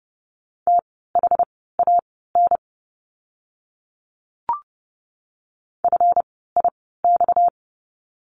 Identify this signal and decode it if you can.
Morse code, heard as a keyed tone, about 30 wpm, spaced as for 17 wpm, 700 Hz. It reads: THAN FIX
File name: course / 30:10wpm / 230 Lesson 37f - Letter x - Words 30wpm